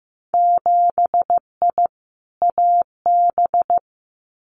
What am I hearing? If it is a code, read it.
Morse code, heard as a keyed tone, about 15 wpm, 700 Hz. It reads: 7I AB
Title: Winter Field Day Exchanges Encore 15wpm